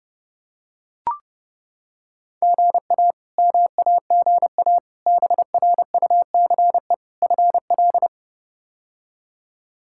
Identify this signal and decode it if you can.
Morse code, heard as a keyed tone, about 30 wpm, 700 Hz. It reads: GA MAGA BRUCE FL
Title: K1USN SST QSO Practice - 30wpm